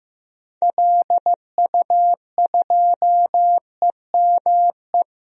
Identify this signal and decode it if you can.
Morse code, heard as a keyed tone, about 15 wpm, 700 Hz. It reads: LU2EME
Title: DXCC-100 Call Signs - 6 Characters - 15wpm